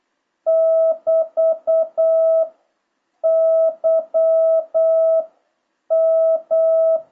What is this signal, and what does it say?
Morse code, heard as a keyed tone, about 8 wpm, 650 Hz. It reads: <BT> Y M